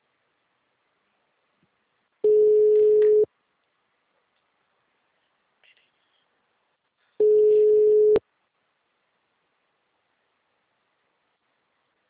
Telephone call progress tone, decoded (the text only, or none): ringback tone